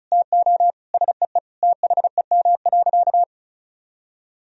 Morse code, heard as a keyed tone, about 36 wpm, 700 Hz. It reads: TO SEE THEM.
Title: Sentences from Top 1000 Words - Review Part 1 - 35wpm